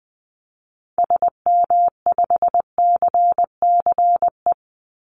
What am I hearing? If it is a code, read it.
Morse code, heard as a keyed tone, about 20 wpm, 700 Hz. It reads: SM5CCE
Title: Call Signs Repeated 3x - 20wpm